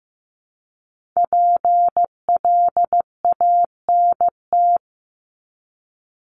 Morse code, heard as a keyed tone, about 15 wpm, 700 Hz. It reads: PLANT
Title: Top 500 Words 15wpm